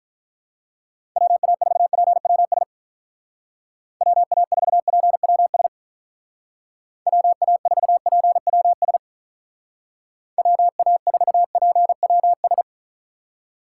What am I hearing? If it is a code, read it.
Morse code, heard as a keyed tone, about 46 wpm, 700 Hz. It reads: WA4PWS WA4PWS WA4PWS WA4PWS